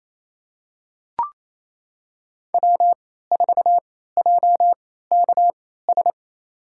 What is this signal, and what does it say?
Morse code, heard as a keyed tone, about 28 wpm, spaced as for 16 wpm, 700 Hz. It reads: W4JKS
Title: USA Call Signs - 3x Character Spacing - 28wpm